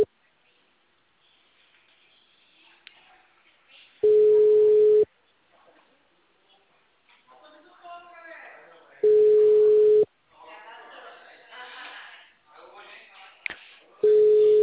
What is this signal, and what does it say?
Ringback tone (Europe, 425 Hz), with speech.